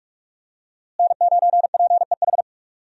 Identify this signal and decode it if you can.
Morse code, heard as a keyed tone, about 46 wpm, 700 Hz. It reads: N9PEH
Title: USA Call Signs - 5 Characters - Repeated 2x - 45wpm